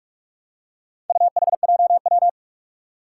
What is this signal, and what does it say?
Morse code, heard as a keyed tone, about 47 wpm, 700 Hz. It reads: UHJW
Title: Four Challenging Letter-Number Combinations 45wpm